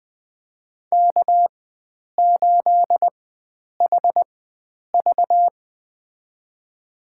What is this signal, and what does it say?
Morse code, heard as a keyed tone, about 20 wpm, spaced as for 9 wpm, 700 Hz. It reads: K8HV
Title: USA Call Signs - 4 Characters - 4x Character Spacing - 20wpm